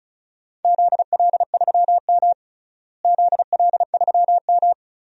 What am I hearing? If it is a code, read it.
Morse code, heard as a keyed tone, about 36 wpm, 700 Hz. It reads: ZL3M ZL3M
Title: DXCC-100 Call Signs - 4 Characters - Repeated 2x - 35wpm